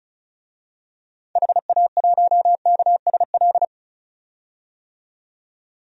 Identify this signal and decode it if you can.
Morse code, heard as a keyed tone, about 34 wpm, 700 Hz. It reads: HA1KSL
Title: DXCC-100 Call Signs - 6 Characters - 35wpm